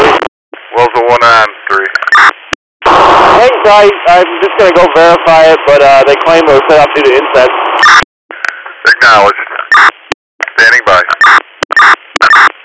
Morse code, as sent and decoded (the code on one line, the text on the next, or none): none